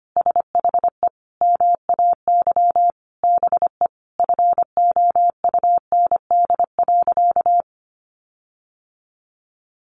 Morse code, heard as a keyed tone, about 25 wpm, 700 Hz. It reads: SHE MAY BE FOUND.
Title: Sentences from Top 1500 Words 25wpm